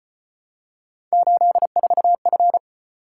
Morse code, heard as a keyed tone, about 34 wpm, 700 Hz. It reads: 84F